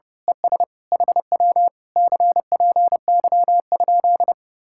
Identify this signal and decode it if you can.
Morse code, heard as a keyed tone, about 29 wpm, 700 Hz. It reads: ES HW CPY?
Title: Speed-Racing Common QSO Phrases Encore - 30wpm